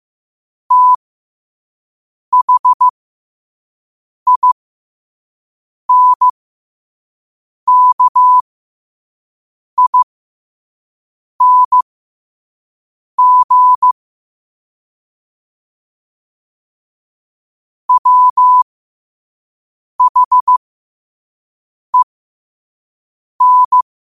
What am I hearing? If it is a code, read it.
Morse code, heard as a keyed tone, about 15 wpm, spaced as for 5 wpm, 1000 Hz. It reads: THINKING WHEN